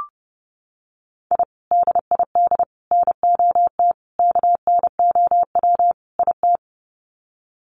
Morse code, heard as a keyed tone, about 30 wpm, 700 Hz. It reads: I DID NOT KNOW IT
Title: Sets of 5 Words Encore2 30wpm